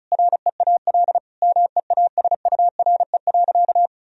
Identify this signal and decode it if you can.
Morse code, heard as a keyed tone, about 34 wpm, 700 Hz. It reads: REAL MEASURE.
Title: Sentences from Top 1000 Words - Review Part 1 - 35wpm